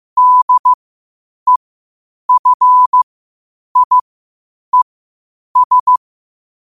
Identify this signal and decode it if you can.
Morse code, heard as a keyed tone, about 15 wpm, spaced as for 8 wpm, 1000 Hz. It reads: DEFIES
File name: qotdfiles / 8wpm